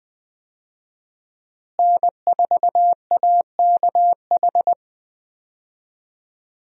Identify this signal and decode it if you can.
Morse code, heard as a keyed tone, about 20 wpm, 700 Hz. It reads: N4AKH
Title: USA Call Signs - 5 Characters - Repeated 3x - 20wpm